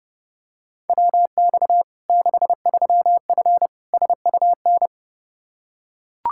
Morse code, heard as a keyed tone, about 32 wpm, 700 Hz. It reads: WX 63F SUN
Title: Practice QSO - Exchange 2 - Info - 30wpm